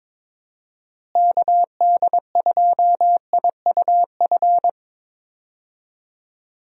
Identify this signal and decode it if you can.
Morse code, heard as a keyed tone, about 22 wpm, 700 Hz. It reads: KD2IUF
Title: USA Call Signs - 6 Characters - Repeated 2x - 22wpm